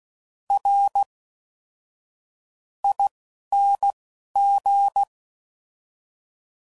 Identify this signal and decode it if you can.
Morse code, heard as a keyed tone, about 16 wpm, spaced as for 11 wpm, 800 Hz. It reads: R ING
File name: Suffixes / Special INGs / Fast INGs